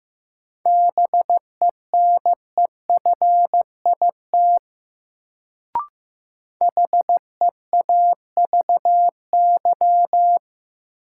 Morse code, heard as a keyed tone, about 15 wpm, 700 Hz. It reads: BENEFIT HEAVY
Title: Sentences from Top 1000 Words - Review Part 6 - New Word Drill - 15wpm